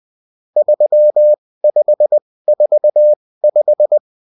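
Morse code, heard as a keyed tone, about 20 wpm, spaced as for 16 wpm, 600 Hz. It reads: 3545
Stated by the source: Morse code bulletin of the UFT.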